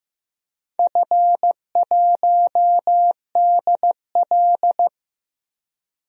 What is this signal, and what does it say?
Morse code, heard as a keyed tone, about 15 wpm, 700 Hz. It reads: F1DL
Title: DXCC-100 Call Signs - 4 Characters - 15wpm